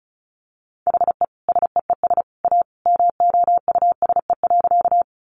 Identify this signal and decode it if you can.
Morse code, heard as a keyed tone, about 34 wpm, 700 Hz. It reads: HE SEES A MOUSE.
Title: Easy Phrases 35wpm